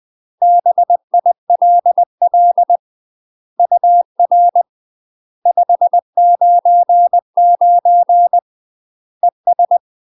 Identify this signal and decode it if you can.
Morse code, heard as a keyed tone, about 20 wpm, 700 Hz. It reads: BILL UR 599 ES
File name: 20WPM